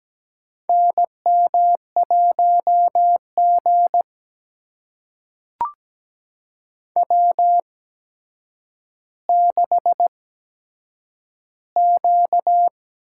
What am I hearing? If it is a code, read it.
Morse code, heard as a keyed tone, about 17 wpm, spaced as for 8 wpm, 700 Hz. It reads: NM1G W6Q
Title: USA Call Signs - 4 Characters - 8x Character Spacing - 17wpm